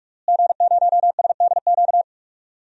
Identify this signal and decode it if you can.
Morse code, heard as a keyed tone, about 46 wpm, 700 Hz. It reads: G0SDQ
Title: DXCC-100 Call Signs - 5 Characters - Repeated 3x - 45wpm